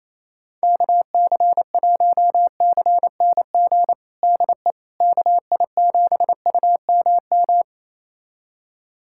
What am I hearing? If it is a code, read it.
Morse code, heard as a keyed tone, about 28 wpm, 700 Hz. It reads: KC1CNG DE KI7UMM
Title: USA Callsign DE Callsign 28wpm